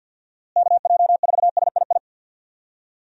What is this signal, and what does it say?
Morse code, heard as a keyed tone, about 49 wpm, 700 Hz. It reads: KJ4SII
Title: USA Call Signs - 6 Characters - Repeated 2x - 50wpm